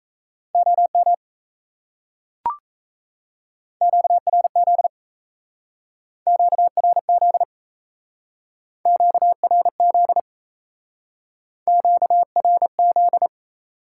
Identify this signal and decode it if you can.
Morse code, heard as a keyed tone, about 36 wpm, 700 Hz. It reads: OM QRZ QRZ QRZ QRZ